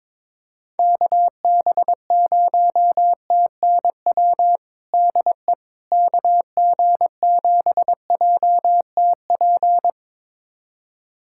Morse code, heard as a keyed tone, about 22 wpm, 700 Hz. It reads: KB0TNW DE KG7JTP
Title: USA Callsign DE Callsign - 6 Characters - 22wpm